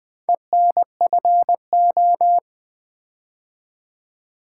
Morse code, heard as a keyed tone, about 20 wpm, 700 Hz. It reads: ENFO
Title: Build A Word - 4 Letters At A Time - 11-Letter Words - 20wpm